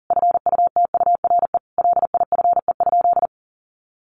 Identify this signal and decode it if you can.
Morse code, heard as a keyed tone, about 40 wpm, 700 Hz. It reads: FUTURE LIFE?
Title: Sentences from Top 1000 Words 40wpm